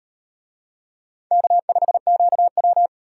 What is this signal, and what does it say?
Morse code, heard as a keyed tone, about 36 wpm, 700 Hz. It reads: K5QW